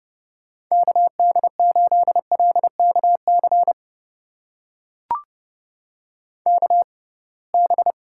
Morse code, heard as a keyed tone, about 30 wpm, spaced as for 23 wpm, 700 Hz. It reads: KD8LKC KB
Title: USA Call Signs - 6 Characters - 6x Character Spacing - 30wpm